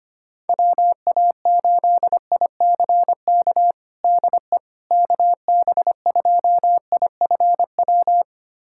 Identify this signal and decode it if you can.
Morse code, heard as a keyed tone, about 25 wpm, 700 Hz. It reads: WA8ICK DE KB2IFW
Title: USA Callsign DE Callsign 25wpm